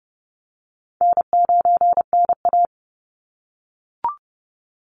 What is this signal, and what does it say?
Morse code, heard as a keyed tone, about 30 wpm, 700 Hz. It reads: N9NA
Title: Easy Call Signs - 30wpm